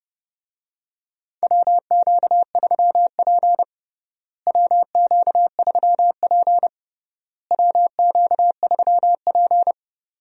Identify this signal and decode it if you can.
Morse code, heard as a keyed tone, about 30 wpm, 700 Hz. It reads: WQ3P WQ3P WQ3P
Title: USA Call Signs - 4 Characters - Repeated 3x - 30wpm